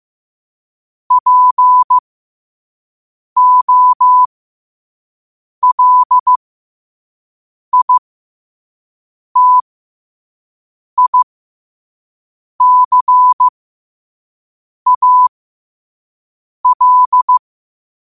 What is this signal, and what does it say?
Morse code, heard as a keyed tone, about 15 wpm, spaced as for 5 wpm, 1000 Hz. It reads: POLITICAL
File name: qotdfiles / 5wpm